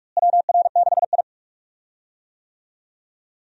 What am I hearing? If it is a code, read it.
Morse code, heard as a keyed tone, about 48 wpm, 700 Hz. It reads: WR6I
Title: USA Call Signs - 4 Characters - Repeated 2x - 45wpm